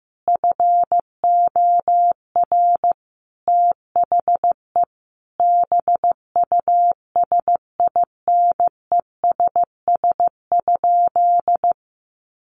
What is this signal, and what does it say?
Morse code, heard as a keyed tone, about 15 wpm, 700 Hz. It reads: FOR THE BUSINESS?
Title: Sentences from Top 1400 Words 15wpm